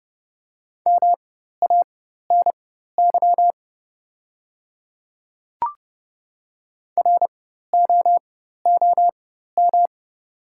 Morse code, heard as a keyed tone, about 30 wpm, spaced as for 14 wpm, 690 Hz. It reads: MANY ROOM
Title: 4x Character Spacing - Top 500 Words - 30wpm